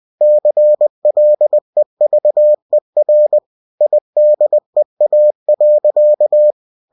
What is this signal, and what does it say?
Morse code, heard as a keyed tone, about 20 wpm, 600 Hz. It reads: CLEVER IDEA.
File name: cw-books / winnie-the-pooh / winnie-20wpm-page-0041